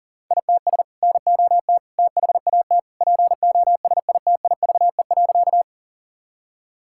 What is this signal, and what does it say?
Morse code, heard as a keyed tone, about 40 wpm, 700 Hz. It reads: ITS NOT THAT POSITIVE.
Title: Sentences from Top 1000 Words - Review Part 5 - 40wpm